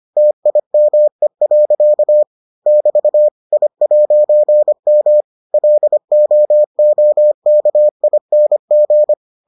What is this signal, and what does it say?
Morse code, heard as a keyed tone, about 25 wpm, 600 Hz. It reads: TIME. <BT> I'M LOOKING